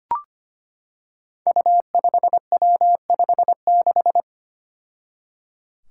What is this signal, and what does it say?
Morse code, heard as a keyed tone, about 25 wpm, 700 Hz. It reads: U5W56
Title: Five Challenging Letter-Number Combinations 25wpm